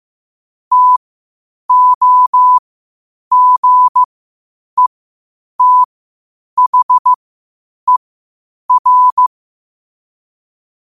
Morse code, heard as a keyed tone, about 15 wpm, spaced as for 8 wpm, 1000 Hz. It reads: TOGETHER
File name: qotdfiles / 8wpm